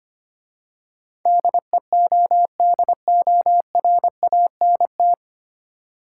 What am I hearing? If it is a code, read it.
Morse code, heard as a keyed tone, about 25 wpm, 690 Hz. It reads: DEODORANT
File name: practice / 25wpm / Easy Words with Hints - 25wpm